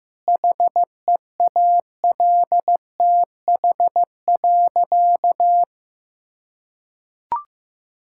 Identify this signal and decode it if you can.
Morse code, heard as a keyed tone, about 15 wpm, 700 Hz. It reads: HEALTH.
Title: Sentences from Top 1000 Words - Review Part 6 - 15wpm